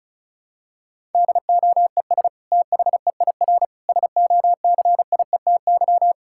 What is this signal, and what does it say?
Morse code, heard as a keyed tone, about 36 wpm, 700 Hz. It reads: DOES THEIR SOCIETY
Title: Sentences from Top 1000 Words - Review Part 6 - 35wpm